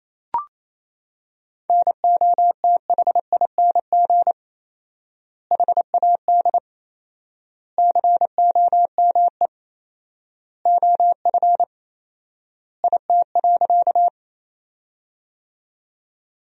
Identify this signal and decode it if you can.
Morse code, heard as a keyed tone, about 28 wpm, 700 Hz. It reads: NOTHING HAD COME OF IT.
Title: 4x Word-Spacing - Sentences from Top 500 Words - 28wpm